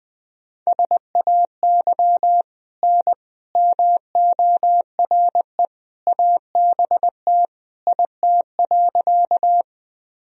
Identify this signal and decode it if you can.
Morse code, heard as a keyed tone, about 20 wpm, 700 Hz. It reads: SAY N MORE ABT IT.
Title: Sentences from Top 200 Words and Abbreviations 20wpm